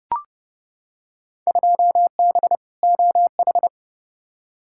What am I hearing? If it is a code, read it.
Morse code, heard as a keyed tone, about 31 wpm, 700 Hz. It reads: 2B OH
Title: Field Day Exchanges Encore - 30wpm